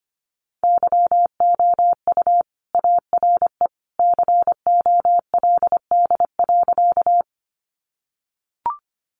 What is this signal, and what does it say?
Morse code, heard as a keyed tone, about 25 wpm, 700 Hz. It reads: YOU ARE COLD.
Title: Sentences from Top 400 Words 25wpm